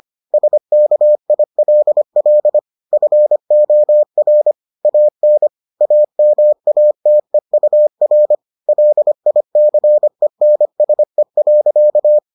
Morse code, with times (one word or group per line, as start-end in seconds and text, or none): 0.3-2.6 SKILL
2.9-4.5 FOR
4.8-5.5 AN
5.8-8.3 AMATEUR
8.7-12.2 LICENSE.